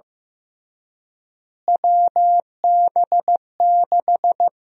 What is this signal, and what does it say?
Morse code, heard as a keyed tone, about 15 wpm, 700 Hz. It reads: WB6
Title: USA Call Sign Prefixes - Repeated 2x - 15wpm